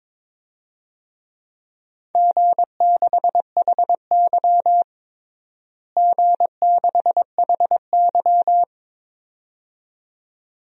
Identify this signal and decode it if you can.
Morse code, heard as a keyed tone, about 22 wpm, 700 Hz. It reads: G6HY G6HY
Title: DXCC-100 Call Signs - 4 Characters - Repeated 2x - 22wpm